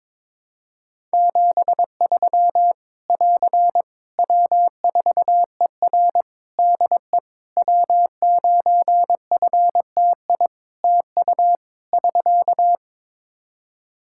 Morse code, heard as a keyed tone, about 22 wpm, 700 Hz. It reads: 73 <AR> W4ER DE W9FTI TU <SK>
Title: Practice QSO - Exchange 4 - Ending - 22wpm